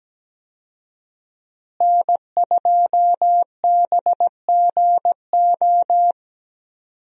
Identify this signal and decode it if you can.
Morse code, heard as a keyed tone, about 17 wpm, 700 Hz. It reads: N2BGO
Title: USA Call Signs - 5 Characters - Repeated 3x - 17wpm